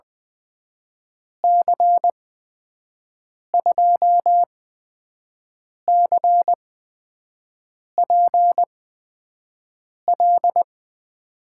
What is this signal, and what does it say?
Morse code, heard as a keyed tone, about 20 wpm, spaced as for 5 wpm, 700 Hz. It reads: C2CPL
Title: USA Call Signs - 6 Characters - 8x Character Spacing - 20wpm